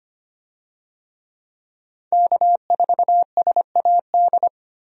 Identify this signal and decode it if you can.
Morse code, heard as a keyed tone, about 25 wpm, 700 Hz. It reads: K4SAD
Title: USA Call Signs - 5 Characters - Repeated 3x - 25wpm